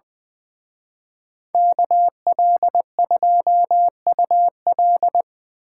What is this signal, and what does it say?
Morse code, heard as a keyed tone, about 24 wpm, 700 Hz. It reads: KL2UL
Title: Speed Racing - USA Call Signs 20wpm